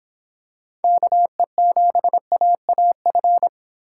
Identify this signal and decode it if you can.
Morse code, heard as a keyed tone, about 26 wpm, 700 Hz. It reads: KE7AAF